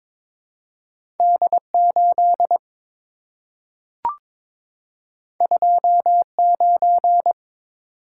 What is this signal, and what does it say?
Morse code, heard as a keyed tone, about 22 wpm, 700 Hz. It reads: D8 29
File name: practice / 22wpm / Two Letter-Number Combinations - 22wpm